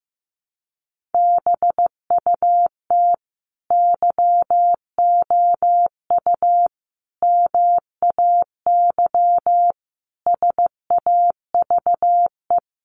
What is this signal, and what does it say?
Morse code, heard as a keyed tone, about 14 wpm, 700 Hz. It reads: BUT YOU MAY SAVE
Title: Sentences from Top 1100 Words 15wpm